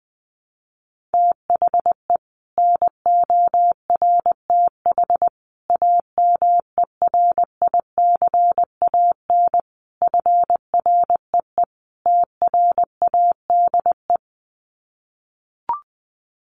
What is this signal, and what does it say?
Morse code, heard as a keyed tone, about 20 wpm, 700 Hz. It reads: THE NORTH AMERICAN FREE TRADE
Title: Sets of 5 Words Encore 20wpm